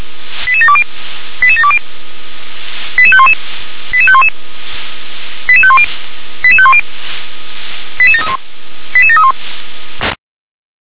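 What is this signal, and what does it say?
Five-tone selective call (ZVEI), heard as quick runs of five tones, tones of about 70 ms. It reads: 88720 88720 88511 88511 88510 88510 88731 88731
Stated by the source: ZVEI five-tone call received on SDR.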